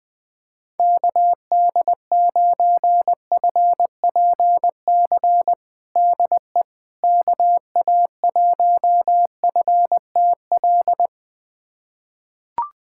Morse code, heard as a keyed tone, about 20 wpm, 700 Hz. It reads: KD9FPC DE KA1FTL
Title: USA Callsign DE Callsign - 6 Characters - 20wpm